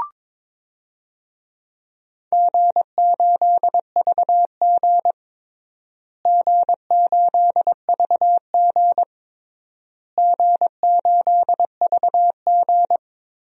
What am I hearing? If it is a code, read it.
Morse code, heard as a keyed tone, about 22 wpm, 700 Hz. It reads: G8VG G8VG G8VG
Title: DXCC-100 Call Signs - 4 Characters - Repeated 3x - 22wpm